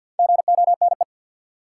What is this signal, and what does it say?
Morse code, heard as a keyed tone, about 52 wpm, 700 Hz. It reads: GONE
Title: Sentences from Top 1200 Words - New Word Drill - 50wpm